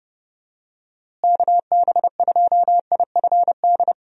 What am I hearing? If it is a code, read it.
Morse code, heard as a keyed tone, about 30 wpm, 700 Hz. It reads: KB2IFD
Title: USA Call Signs - 6 Characters - Repeated 2x - 30wpm